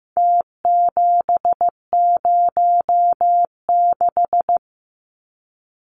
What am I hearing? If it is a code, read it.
Morse code, heard as a keyed tone, about 15 wpm, 700 Hz. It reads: T706
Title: Numbers - Contest Style - 4-Digit 15wpm